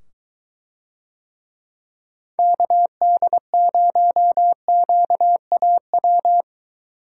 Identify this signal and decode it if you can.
Morse code, heard as a keyed tone, about 22 wpm, 700 Hz. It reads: KD0QAW